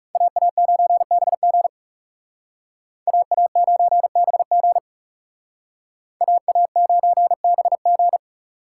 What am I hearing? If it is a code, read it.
Morse code, heard as a keyed tone, about 38 wpm, 700 Hz. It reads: AA9BG AA9BG AA9BG